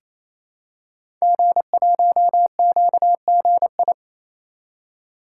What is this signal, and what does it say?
Morse code, heard as a keyed tone, about 28 wpm, 700 Hz. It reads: G1QGI
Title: Code Groups - 5 Letter:Numbers - 28wpm